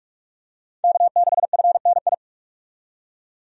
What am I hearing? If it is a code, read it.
Morse code, heard as a keyed tone, about 46 wpm, 700 Hz. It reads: K6FNI
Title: USA Call Signs - 5 Characters - 45wpm